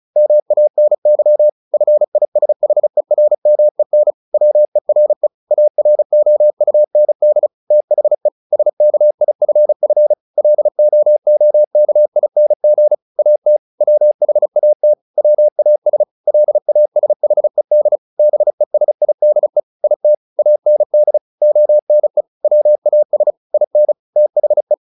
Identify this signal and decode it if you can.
Morse code, heard as a keyed tone, about 34 wpm, 600 Hz. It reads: MANY FISHERMEN WERE AROUND THE SKIFF LOOKING AT WHAT WAS LASHED BESIDE IT AND ONE WAS IN THE
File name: the old man and the sea 2 35wpm